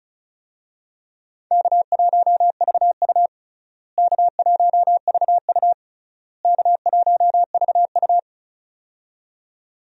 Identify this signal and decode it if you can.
Morse code, heard as a keyed tone, about 34 wpm, 700 Hz. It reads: K1VU K1VU K1VU
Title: USA Call Signs - 4 Characters - Repeated 3x - 35wpm